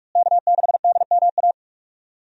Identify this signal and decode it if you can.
Morse code, heard as a keyed tone, about 46 wpm, 700 Hz. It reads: K6DMA
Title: USA Call Signs - 5 Characters - Repeated 2x - 45wpm